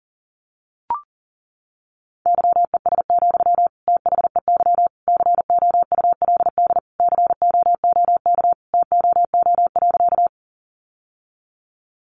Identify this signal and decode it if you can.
Morse code, heard as a keyed tone, about 40 wpm, 700 Hz. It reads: YES, THEY COULD COOK TOO.
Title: Sentences from Top 700 Words 40wpm